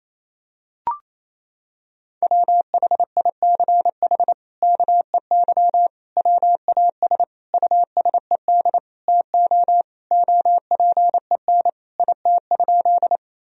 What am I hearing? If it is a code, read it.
Morse code, heard as a keyed tone, about 28 wpm, 700 Hz. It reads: WHICH KEY WAS USED TO OPEN IT?
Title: Sentences from Top 1000 Words - Review Part 4 - 28wpm